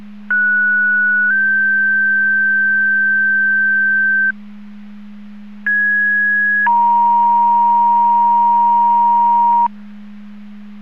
Two-tone page, heard as a one-second tone, then a three-second tone, about 1470 then 1550 Hz, about 1650 then 940 Hz. It is heard from 0.3 to 9.7 s.